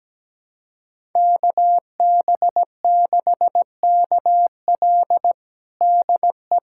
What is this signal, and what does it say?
Morse code, heard as a keyed tone, about 17 wpm, 700 Hz. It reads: KB6KL DE
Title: USA Callsign DE Callsign 17wpm